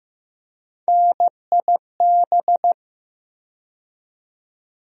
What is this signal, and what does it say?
Morse code, heard as a keyed tone, about 15 wpm, 700 Hz. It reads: NIB
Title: USA Call Sign Postfixes - 3 Letters 15wpm